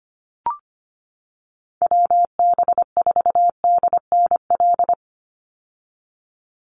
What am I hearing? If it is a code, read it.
Morse code, heard as a keyed tone, about 25 wpm, 700 Hz. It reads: WB4DNL
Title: Call Signs - 25wpm